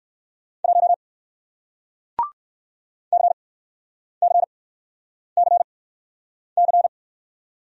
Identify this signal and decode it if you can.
Morse code, heard as a keyed tone, about 56 wpm, spaced as for 9 wpm, 700 Hz. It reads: 1 CCCC
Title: Speed-Racing Single Letter-Number 45wpm